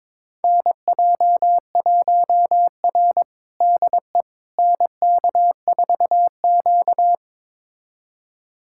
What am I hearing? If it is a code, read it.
Morse code, heard as a keyed tone, about 22 wpm, 700 Hz. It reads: NJ1R DE NK4Q
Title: USA Callsign DE Callsign - 4 Characters - 22wpm